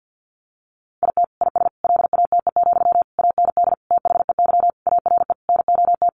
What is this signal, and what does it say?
Morse code, heard as a keyed tone, about 50 wpm, 700 Hz. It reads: IT IS LATE, AND THEY ARE NOT
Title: Sentences from Top 600 Words 50wpm